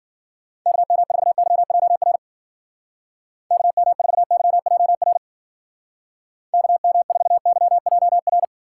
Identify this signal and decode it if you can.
Morse code, heard as a keyed tone, about 51 wpm, 700 Hz. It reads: KM4YJR KM4YJR KM4YJR